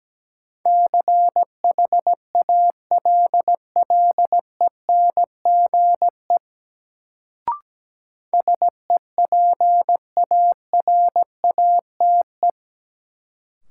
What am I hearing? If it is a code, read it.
Morse code, heard as a keyed tone, about 17 wpm, 700 Hz. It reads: CHALLENGE SEPARATE
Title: Sentences from Top 1000 Words - Review Part 7 - New Word Drill - 17wpm